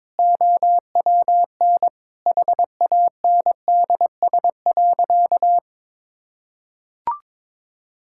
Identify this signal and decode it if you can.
Morse code, heard as a keyed tone, about 22 wpm, 700 Hz. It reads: OWN HANDS.
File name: practice / 22wpm / Sentences with 8-Word Phrases - 22wpm